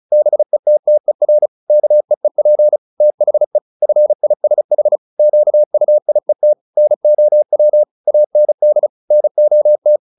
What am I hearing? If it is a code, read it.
Morse code, heard as a keyed tone, about 34 wpm, 600 Hz. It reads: BETTER KEEP THE FISH QUIET NOW AND NOT
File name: the old man and the sea 2 35wpm